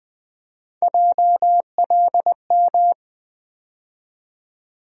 Morse code, heard as a keyed tone, about 20 wpm, 700 Hz. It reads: JLM
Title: USA Call Sign Postfixes - 3 Letters 20wpm